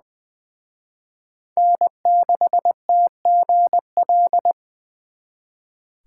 Morse code, heard as a keyed tone, about 23 wpm, 700 Hz. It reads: N6TGL
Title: Speed Racing - USA Call Signs 20wpm